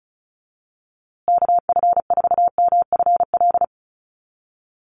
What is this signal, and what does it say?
Morse code, heard as a keyed tone, about 34 wpm, 700 Hz. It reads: KF4MFL
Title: USA Call Signs Repeated 2x 35wpm